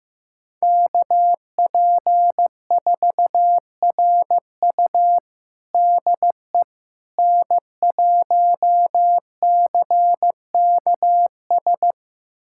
Morse code, heard as a keyed tone, about 15 wpm, 700 Hz. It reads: KP4RU DE N1CKS
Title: USA Callsign DE Callsign - 5 Characters - 15wpm